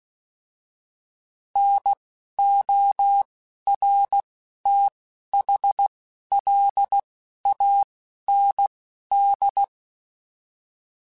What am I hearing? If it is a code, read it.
Morse code, heard as a keyed tone, about 16 wpm, spaced as for 11 wpm, 800 Hz. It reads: NORTHLAND